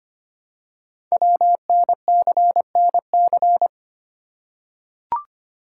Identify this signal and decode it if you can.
Morse code, heard as a keyed tone, about 25 wpm, 700 Hz. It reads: WNCNC
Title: Code Groups - 5 Letters - 25wpm